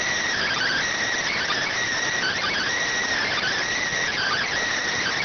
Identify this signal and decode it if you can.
Emergency Alert System (SAME) tones, heard as harsh buzzing bursts. It is heard from 0.4 to 4.5 s.